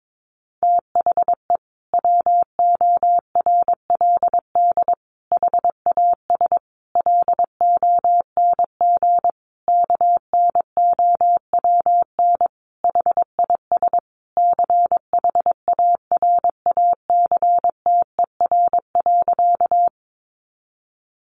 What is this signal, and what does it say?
Morse code, heard as a keyed tone, about 22 wpm, 700 Hz. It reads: THE WORLD HAS LONG KNOWN HIS CHARACTER.
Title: Sentences from Top 1200 Words 22wpm